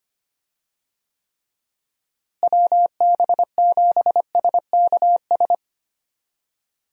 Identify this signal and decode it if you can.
Morse code, heard as a keyed tone, about 25 wpm, 700 Hz. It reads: WB7SKS
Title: USA Call Signs - 6 Characters - Repeated 3x - 25wpm